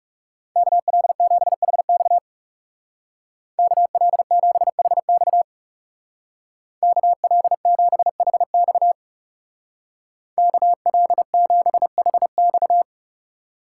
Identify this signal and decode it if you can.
Morse code, heard as a keyed tone, about 37 wpm, 700 Hz. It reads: KL7HX KL7HX KL7HX KL7HX
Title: Speed Racing - USA 2x2 Call Signs 30wpm